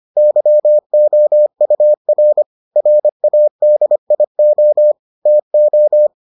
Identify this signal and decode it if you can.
Morse code, heard as a keyed tone, about 25 wpm, 600 Hz. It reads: YOUR RADIO TO